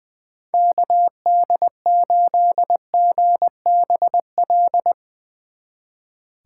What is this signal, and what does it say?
Morse code, heard as a keyed tone, about 21 wpm, 700 Hz. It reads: KD8GBL
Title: Speed Racing - USA 2x3 Call Signs 20wpm